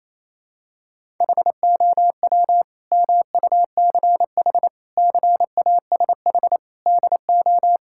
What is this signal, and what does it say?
Morse code, heard as a keyed tone, about 28 wpm, 700 Hz. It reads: HOW MUCH CASH DO
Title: Sentences from Top 1000 Words - Review Part 5 - 28wpm